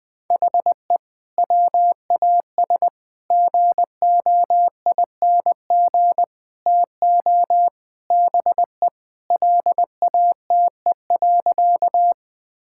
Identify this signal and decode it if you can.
Morse code, heard as a keyed tone, about 20 wpm, 700 Hz. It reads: HE WAS GOING TO BE LATE.
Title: Sentences with 5-Word Phrases - 20wpm